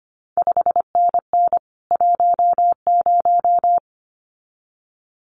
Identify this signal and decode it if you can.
Morse code, heard as a keyed tone, about 25 wpm, 700 Hz. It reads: 5NN 10
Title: CQ WW DX Contest Exchange Encore - 25wpm